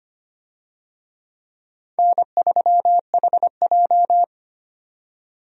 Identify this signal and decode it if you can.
Morse code, heard as a keyed tone, about 25 wpm, 700 Hz. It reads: N3HJ
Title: USA Call Signs - 4 Characters - 25wpm